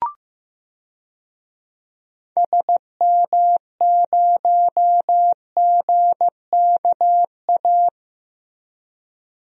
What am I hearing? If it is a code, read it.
Morse code, heard as a keyed tone, about 15 wpm, 700 Hz. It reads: SM0GKA
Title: DXCC-100 Call Signs - 6 Characters - Repeated 3x - 15wpm